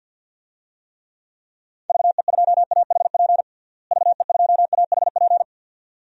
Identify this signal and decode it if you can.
Morse code, heard as a keyed tone, about 53 wpm, 700 Hz. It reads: VE2AHP VE2AHP
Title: DXCC-100 Call Signs - 6 Characters - Repeated 2x - 50wpm